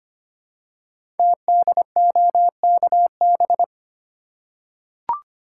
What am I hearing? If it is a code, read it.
Morse code, heard as a keyed tone, about 25 wpm, 700 Hz. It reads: TDOKB
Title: Code Groups - 5 Letters - 25wpm